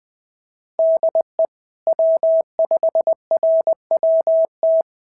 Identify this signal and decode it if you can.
Morse code, heard as a keyed tone, about 19 wpm, 650 Hz. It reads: DE W5RWT
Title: SOTA QSO Practice 20wpm